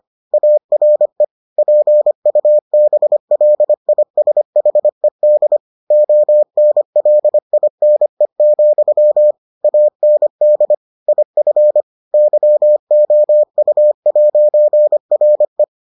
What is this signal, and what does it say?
Morse code, heard as a keyed tone, about 25 wpm, 600 Hz. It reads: ARE PUBLISHED ONLINE, AND IF YOU'RE